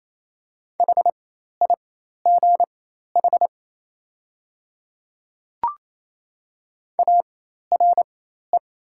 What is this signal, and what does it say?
Morse code, heard as a keyed tone, about 28 wpm, spaced as for 13 wpm, 700 Hz. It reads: HIGH ARE
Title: 4x Character Spacing - Top 200 Words - 28wpm